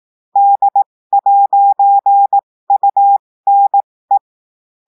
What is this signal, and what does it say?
Morse code, heard as a keyed tone, about 18 wpm, spaced as for 15 wpm, 800 Hz. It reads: D'UNE
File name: cours-du-lundi files / cours-1580-texte-clair-15-mpm